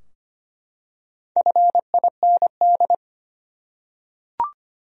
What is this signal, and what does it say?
Morse code, heard as a keyed tone, about 25 wpm, 700 Hz. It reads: FIND
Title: Top 100 Words 25wpm